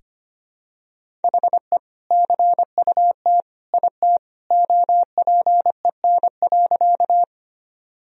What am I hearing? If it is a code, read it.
Morse code, heard as a keyed tone, about 25 wpm, 700 Hz. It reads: HE CUT IT OPEN.
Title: Sentences from Top 1000 Words - Review Part 1 - 25wpm